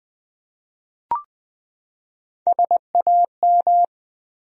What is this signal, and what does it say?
Morse code, heard as a keyed tone, about 20 wpm, 700 Hz. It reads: SAM
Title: USA Common Names - Max 4 Letters - 20wpm